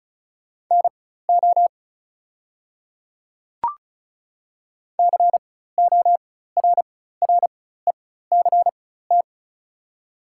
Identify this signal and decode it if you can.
Morse code, heard as a keyed tone, about 34 wpm, spaced as for 16 wpm, 690 Hz. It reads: NO CORRECT
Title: 4x Character Spacing - Top 500 Words - 35wpm